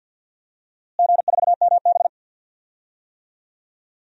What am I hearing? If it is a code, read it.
Morse code, heard as a keyed tone, about 51 wpm, 700 Hz. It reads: G4MB
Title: DXCC-100 Call Signs - 4 Characters - Repeated 2x - 50wpm